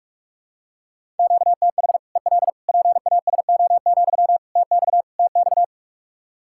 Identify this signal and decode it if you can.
Morse code, heard as a keyed tone, about 46 wpm, 700 Hz. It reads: QTH EL PASO, TX TX
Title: Practice QSO - Exchange 1 - Beginning - 45wpm